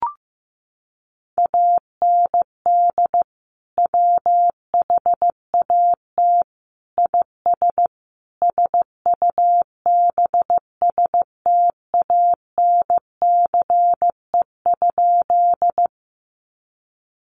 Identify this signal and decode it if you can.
Morse code, heard as a keyed tone, about 15 wpm, 700 Hz. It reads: AND WHAT IS SUBSTANCE?
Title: Sentences from Top 1500 Words 15wpm